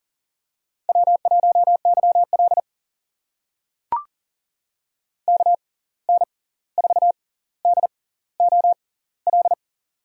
Morse code, heard as a keyed tone, about 38 wpm, spaced as for 18 wpm, 700 Hz. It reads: W1YL KN4DOL
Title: USA Call Signs - 6x Character Spacing - 40wpm